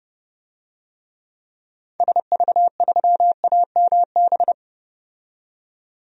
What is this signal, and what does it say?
Morse code, heard as a keyed tone, about 30 wpm, 700 Hz. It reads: SV3AMB
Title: DXCC-100 Call Signs - 6 Characters - 30wpm